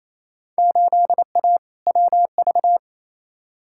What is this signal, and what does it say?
Morse code, heard as a keyed tone, about 28 wpm, 700 Hz. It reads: 8A WV
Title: Field Day Exchanges Encore - 28wpm